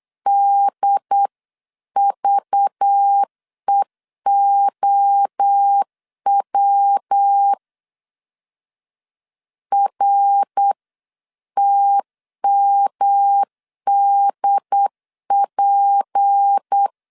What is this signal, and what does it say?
Morse code, heard as a keyed tone, about 8 wpm, 800 Hz. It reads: DVEOW RTMDP
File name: CW / DET / 1 6 1500C S B 8wpm 5142